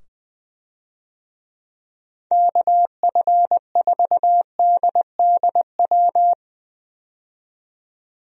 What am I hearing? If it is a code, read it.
Morse code, heard as a keyed tone, about 21 wpm, 700 Hz. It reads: KF4DDW
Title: USA Call Signs - 6 Characters - 20wpm